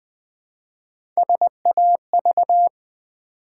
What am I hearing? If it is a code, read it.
Morse code, heard as a keyed tone, about 20 wpm, 700 Hz. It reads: SAV
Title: Build A Word - 1 Letter At A Time - 4-Letter Words - 20wpm